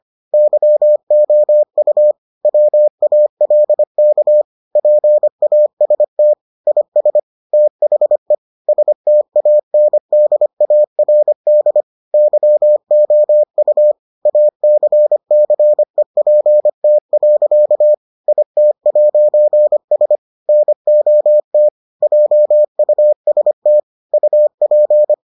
Morse code, heard as a keyed tone, about 25 wpm, 600 Hz. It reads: YOU WALK PAST IS THE STANDARD YOU ACCEPT. IT'S NOT JUST UP